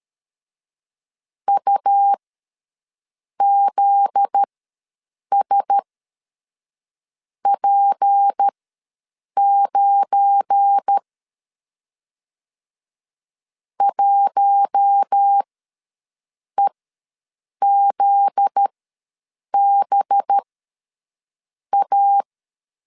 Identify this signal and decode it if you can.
Morse code, heard as a keyed tone, about 13 wpm, spaced as for 6 wpm, 800 Hz. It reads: UZSP9 1EZBA
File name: 7 GF9Z ws 12wpm